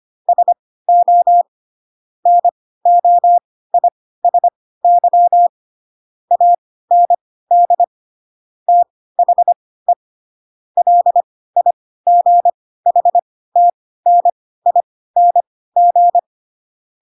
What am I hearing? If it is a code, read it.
Morse code, heard as a keyed tone, about 25 wpm, spaced as for 16 wpm, 700 Hz. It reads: SO NOISY AND THE LIGHTNING
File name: Pinocchio / Pinocchio ch7 10 25 ik1hge